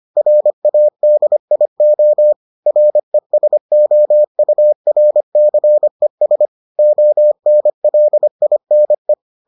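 Morse code, heard as a keyed tone, about 25 wpm, 600 Hz. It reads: RADIO RESOURCES ONLINE
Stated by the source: Morse edition of the Foundations of Amateur Radio episodes.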